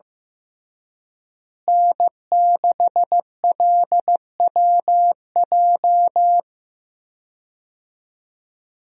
Morse code, heard as a keyed tone, about 15 wpm, 700 Hz. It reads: N6LWJ
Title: USA Call Signs - 5 Characters - Repeated 3x - 15wpm